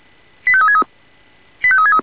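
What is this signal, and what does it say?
Five-tone selective call (ZVEI), heard as quick runs of five tones, tones of about 70 ms. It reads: 85363
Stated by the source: ZVEI five-tone selective call received on SDR.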